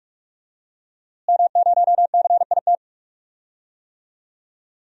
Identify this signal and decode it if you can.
Morse code, heard as a keyed tone, about 46 wpm, 700 Hz. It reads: M0CIT
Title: DXCC-100 Call Signs - 5 Characters - Repeated 3x - 45wpm